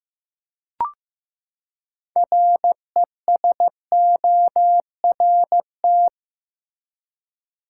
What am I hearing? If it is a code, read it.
Morse code, heard as a keyed tone, about 15 wpm, 700 Hz. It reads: RESORT
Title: Six Letter Words - 15wpm